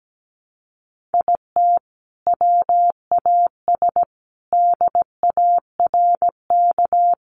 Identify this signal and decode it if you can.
Morse code, heard as a keyed tone, about 17 wpm, 700 Hz. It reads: IT WAS DARK
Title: Sentences from Top 1400 Words 17wpm